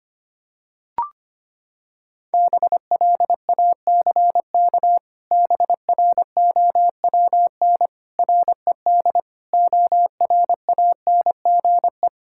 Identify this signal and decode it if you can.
Morse code, heard as a keyed tone, about 25 wpm, 700 Hz. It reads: BLACK BROWN RED ORANGE
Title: Pangram Challenge v2 - 25wpm